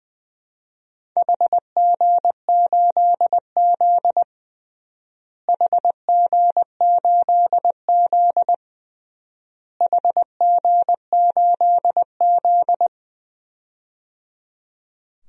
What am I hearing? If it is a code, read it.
Morse code, heard as a keyed tone, about 20 wpm, 700 Hz. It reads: HG8Z HG8Z HG8Z
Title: DXCC-100 Call Signs - 4 Characters - Repeated 3x - 20wpm